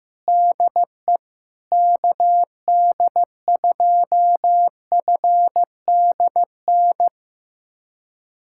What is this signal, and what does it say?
Morse code, heard as a keyed tone, about 15 wpm, 700 Hz. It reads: DE KD2FDN
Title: USA Callsign DE Callsign 15wpm